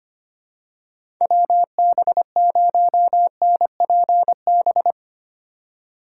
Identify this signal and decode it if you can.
Morse code, heard as a keyed tone, about 25 wpm, 700 Hz. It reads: WB0NPB
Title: USA Call Signs - 6 Characters - 3x Character Spacing - 25wpm